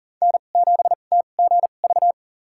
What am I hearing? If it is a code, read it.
Morse code, heard as a keyed tone, about 40 wpm, spaced as for 29 wpm, 700 Hz. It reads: N7TGV
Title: USA Call Signs - 2x Character Spacing - 40wpm